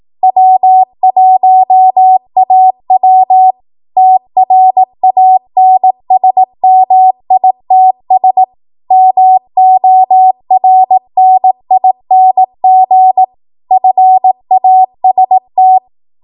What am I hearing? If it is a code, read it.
Morse code, heard as a keyed tone, about 18 wpm, 750 Hz. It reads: W1AW TRANSMITS MORNING FAST
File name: Archive / 18 WPM / 18WPM